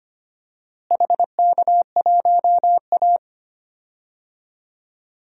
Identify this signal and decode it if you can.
Morse code, heard as a keyed tone, about 25 wpm, 700 Hz. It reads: HK1A
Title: DXCC-100 Call Signs - 4 Characters - Repeated 2x - 25wpm